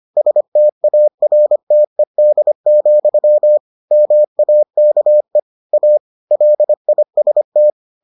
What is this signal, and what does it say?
Morse code, heard as a keyed tone, about 25 wpm, 600 Hz. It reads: STARTED, MAKE A LIST